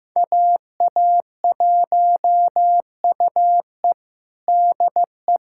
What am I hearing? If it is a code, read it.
Morse code, heard as a keyed tone, about 15 wpm, 700 Hz. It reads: AA1UE DE
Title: USA Callsign DE Callsign - 5 Characters - 15wpm